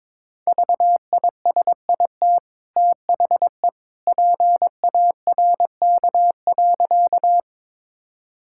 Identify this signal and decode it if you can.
Morse code, heard as a keyed tone, about 22 wpm, 690 Hz. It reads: VISIT THE PARK.
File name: practice / 22wpm / Easy Sentences with Hints - 22wpm